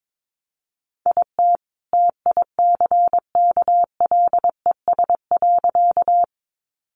Morse code, heard as a keyed tone, about 22 wpm, 700 Hz. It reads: IT TICKLES.
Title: Additional Easy Phrases 22wpm